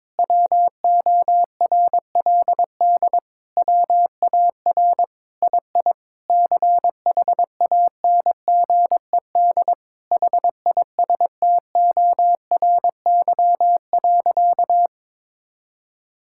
Morse code, heard as a keyed tone, about 22 wpm, 700 Hz. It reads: WORLD WAR II CHANGED HISTORY.
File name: Sentences with 7-Word Phrases - 22wpm